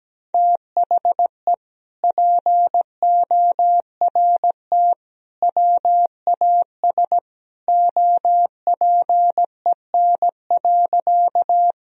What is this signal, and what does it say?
Morse code, heard as a keyed tone, about 17 wpm, 700 Hz. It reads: THE PORT WAS OPEN.
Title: Sentences from Top 1000 Words - Review Part 1 - 17wpm